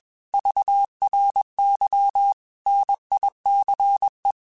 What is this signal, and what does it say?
Morse code, heard as a keyed tone, about 21 wpm, 800 Hz. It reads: VRY NICE
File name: QSOs / QSOs-mp3s / kc7sac@20